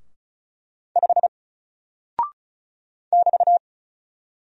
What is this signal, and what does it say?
Morse code, heard as a keyed tone, about 36 wpm, 700 Hz. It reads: 5 <BT>
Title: Single Character (Letter:Number:Punctuation:Prosign) Alt 35wpm